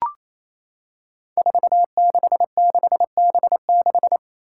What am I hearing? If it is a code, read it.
Morse code, heard as a keyed tone, about 27 wpm, 700 Hz. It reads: 466B6
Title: Five Challenging Letter-Number Combinations 28wpm